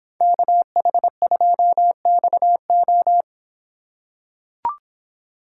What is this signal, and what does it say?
Morse code, heard as a keyed tone, about 24 wpm, 700 Hz. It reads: KH2XO